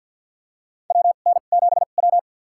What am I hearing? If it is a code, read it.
Morse code, heard as a keyed tone, about 52 wpm, spaced as for 37 wpm, 700 Hz. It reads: WN7W
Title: USA Call Signs - 4 Characters - 2x Character Spacing - 50wpm